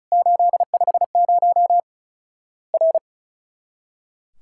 Morse code, heard as a keyed tone, about 36 wpm, 700 Hz. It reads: 850 R
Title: Worked All Europe DX Contest - QTC Exchange - 35wpm